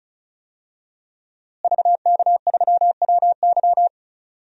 Morse code, heard as a keyed tone, about 36 wpm, 700 Hz. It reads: VK3WY
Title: DXCC-100 Call Signs - 5 Characters - Repeated 2x - 35wpm